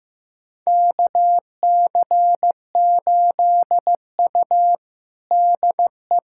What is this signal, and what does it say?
Morse code, heard as a keyed tone, about 16 wpm, 700 Hz. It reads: KC8U DE
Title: USA Callsign DE Callsign - 4 Characters - 15wpm